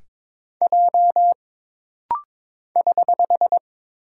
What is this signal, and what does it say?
Morse code, heard as a keyed tone, about 22 wpm, 700 Hz. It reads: J <HH>
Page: This rapid-fire timing is geared to push your ICR (Instant Character Recognition) skills.